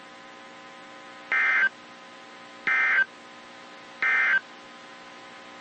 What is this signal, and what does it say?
Emergency Alert System (SAME) tones, heard as harsh buzzing bursts. It is heard from 1.2 to 4.5 s.
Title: WZ2523 - NOAA Weather Radio